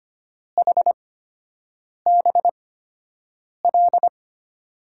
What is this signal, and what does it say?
Morse code, heard as a keyed tone, about 25 wpm, spaced as for 7 wpm, 700 Hz. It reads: HBL
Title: USA Call Signs - 6 Characters - 8x Character Spacing - 25wpm